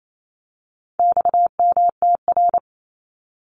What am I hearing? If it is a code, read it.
Morse code, heard as a keyed tone, about 28 wpm, 700 Hz. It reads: XMTR
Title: Abbreviations 28wpm